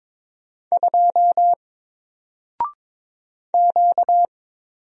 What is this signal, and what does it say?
Morse code, heard as a keyed tone, about 22 wpm, 700 Hz. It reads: 2 Q
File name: Single-Letter-Number-22wpm